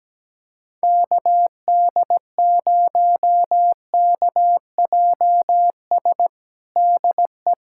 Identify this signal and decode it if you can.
Morse code, heard as a keyed tone, about 17 wpm, 700 Hz. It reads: KD0KJS DE
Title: USA Callsign DE Callsign - 6 Characters - 17wpm